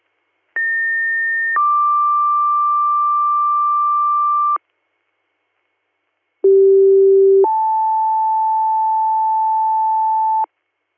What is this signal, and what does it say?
Two-tone page, heard as a one-second tone, then a three-second tone, about 1800 then 1180 Hz, about 390 then 880 Hz. It is heard from 0.6 to 10.4 s.